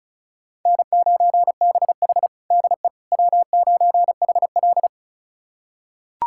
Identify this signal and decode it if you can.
Morse code, heard as a keyed tone, about 36 wpm, 700 Hz. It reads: N9BH DE W9HL
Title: USA Callsign DE Callsign - 4 Characters - 35wpm